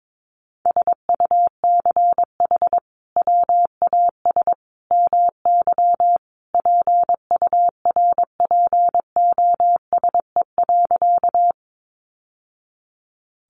Morse code, heard as a keyed tone, about 22 wpm, 700 Hz. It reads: SUCH WAS MY PURPOSE.
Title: Sentences from Top 800 Words 22wpm